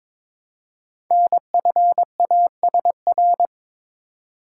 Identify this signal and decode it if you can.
Morse code, heard as a keyed tone, about 22 wpm, 700 Hz. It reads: NFASR
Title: Code Groups - 5 Letters - 22wpm